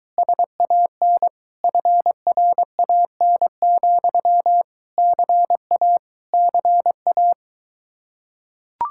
Morse code, heard as a keyed tone, about 23 wpm, 700 Hz. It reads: SAN FRAN, CA CA